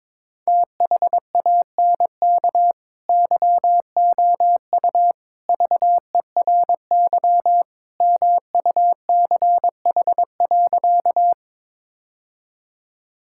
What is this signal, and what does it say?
Morse code, heard as a keyed tone, about 22 wpm, 700 Hz. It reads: THANK YOU VERY MUCH.
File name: practice / 22wpm / Sentences with 3-Word Phrases - 22wpm